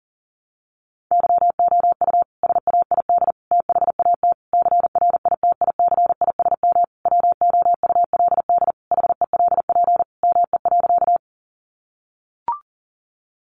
Morse code, heard as a keyed tone, about 40 wpm, 700 Hz. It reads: YOU SAID THAT CRITICISM WOULD HELP ME.
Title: Sentences from Top 1300 Words 40wpm